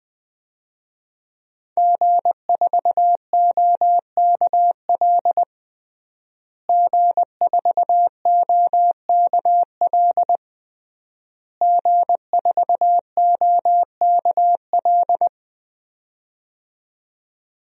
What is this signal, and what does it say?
Morse code, heard as a keyed tone, about 20 wpm, 700 Hz. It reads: G4OKL G4OKL G4OKL
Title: DXCC-100 Call Signs - 5 Characters - Repeated 3x - 20wpm